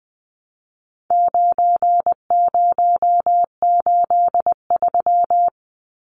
Morse code, heard as a keyed tone, about 19 wpm, 700 Hz. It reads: 9083
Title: Numbers - 4-Digit 20wpm